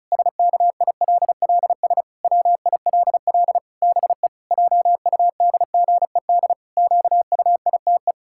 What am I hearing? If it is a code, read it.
Morse code, heard as a keyed tone, about 34 wpm, 700 Hz. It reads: SKILLS WILL BE JUDGED QUITE
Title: Pangram Challenge v2 - 35wpm